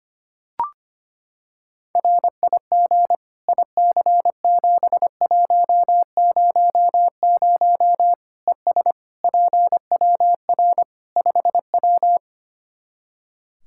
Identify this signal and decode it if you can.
Morse code, heard as a keyed tone, about 25 wpm, 700 Hz. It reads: RIG IC7100 ES PWR 5W
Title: Practice QSO - Exchange 2 - Info - 25wpm